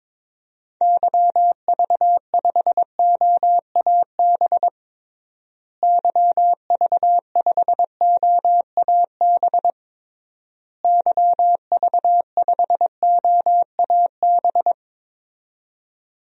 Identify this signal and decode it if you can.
Morse code, heard as a keyed tone, about 22 wpm, 700 Hz. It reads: YV5OAB YV5OAB YV5OAB
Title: DXCC-100 Call Signs - 6 Characters - Repeated 3x - 22wpm